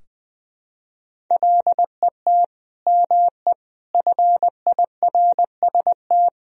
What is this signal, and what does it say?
Morse code, heard as a keyed tone, about 20 wpm, 700 Hz. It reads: LET ME FIRST
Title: Sentences from Top 1000 Words - Review Part 2 - 20wpm